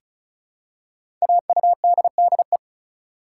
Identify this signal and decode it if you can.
Morse code, heard as a keyed tone, about 36 wpm, 700 Hz. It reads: AUDDE
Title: Code Groups - 5 Letters - 35wpm